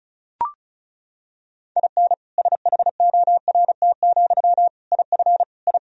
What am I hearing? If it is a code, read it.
Morse code, heard as a keyed tone, about 36 wpm, 700 Hz. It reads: IN SHORT, IF I
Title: Sentences from Top 1000 Words - Review Part 2 - 35wpm